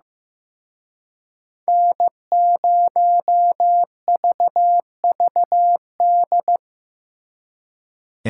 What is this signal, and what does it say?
Morse code, heard as a keyed tone, about 18 wpm, 700 Hz. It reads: N0VVD
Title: Speed Racing - USA 1x3 Call Signs 15wpm